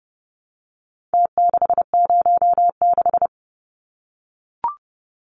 Morse code, heard as a keyed tone, about 30 wpm, 700 Hz. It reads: T606
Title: Numbers - Contest Style - 4-Digit 30wpm